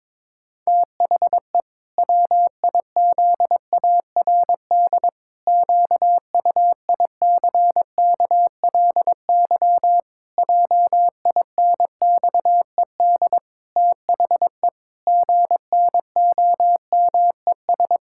Morse code, heard as a keyed tone, about 22 wpm, 700 Hz. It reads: THE WIZARD QUICKLY JINXED THE GNOMES
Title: Pangram Challenge v2 - 22wpm